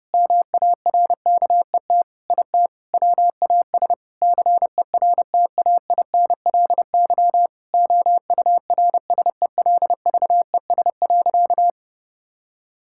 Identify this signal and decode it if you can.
Morse code, heard as a keyed tone, about 30 wpm, 700 Hz. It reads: MARKET IT WAS CERTAINLY OURSELVES.
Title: Sentences from Top 1600 Words 30wpm